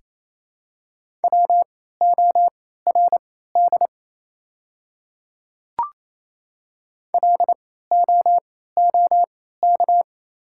Morse code, heard as a keyed tone, about 28 wpm, spaced as for 16 wpm, 700 Hz. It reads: WORD LOOK
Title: 3x Character Spacing - Top 100 Words - 28wpm